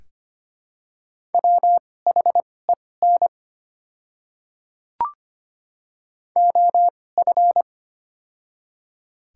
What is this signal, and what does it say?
Morse code, heard as a keyed tone, about 25 wpm, spaced as for 18 wpm, 700 Hz. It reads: WHEN OF
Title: 2x Character Spacing - Top 100 Words - 25wpm